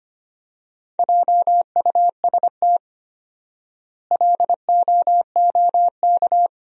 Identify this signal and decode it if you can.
Morse code, heard as a keyed tone, about 25 wpm, 700 Hz. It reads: JUST LOOK
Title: 4x Word-Spacing - Sentences from Top 200 Words - 25wpm